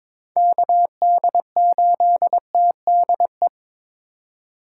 Morse code, heard as a keyed tone, about 22 wpm, 700 Hz. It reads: KD8TDE
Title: USA Call Signs - 6 Characters - 3x Character Spacing - 22wpm